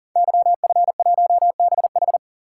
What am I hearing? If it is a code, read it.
Morse code, heard as a keyed tone, about 39 wpm, 690 Hz. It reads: YF1BH
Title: Code Groups - 5 Letter:Numbers - 40wpm